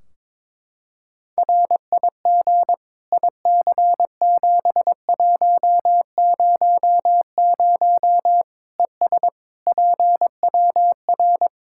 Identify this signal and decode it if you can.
Morse code, heard as a keyed tone, about 21 wpm, 700 Hz. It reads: RIG IC7100 ES PWR
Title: Practice QSO - Exchange 2 - Info - 22wpm